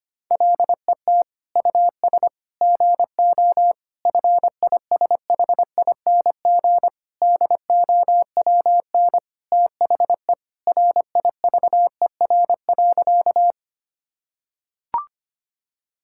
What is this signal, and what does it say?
Morse code, heard as a keyed tone, about 25 wpm, 700 Hz. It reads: LET US GO FISHING DOWN THE RIVER.
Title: Sentences from Top 1000 Words - Review Part 5 - 25wpm